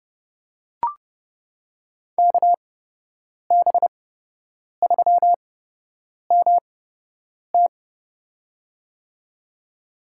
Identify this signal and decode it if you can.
Morse code, heard as a keyed tone, about 30 wpm, spaced as for 8 wpm, 700 Hz. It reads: KB3MT
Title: USA Call Signs - 8x Character Spacing - 30wpm